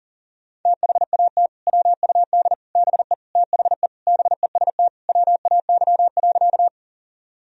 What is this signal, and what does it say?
Morse code, heard as a keyed tone, about 40 wpm, 700 Hz. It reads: THAT WUD BE THE BEST WAY.
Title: Sentences from Top 400 Words and Abbreviations 40wpm